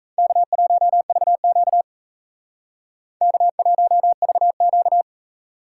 Morse code, heard as a keyed tone, about 35 wpm, 700 Hz. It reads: K1VQ K1VQ